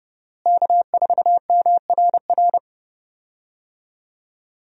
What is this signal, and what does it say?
Morse code, heard as a keyed tone, about 30 wpm, 700 Hz. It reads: K4MRR
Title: USA Call Signs - 5 Characters - Repeated 3x - 30wpm